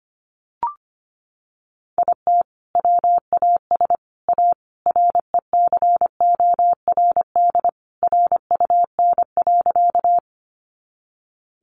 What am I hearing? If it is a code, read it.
Morse code, heard as a keyed tone, about 25 wpm, 700 Hz. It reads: IT WAS A RECORD RUN.
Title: Sentences from Top 500 Words 25wpm